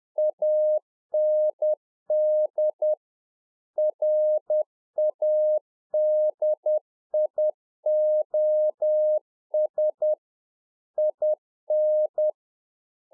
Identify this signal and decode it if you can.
Morse code, heard as a keyed tone, about 10 wpm, 610 Hz. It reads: AND RADIOS IN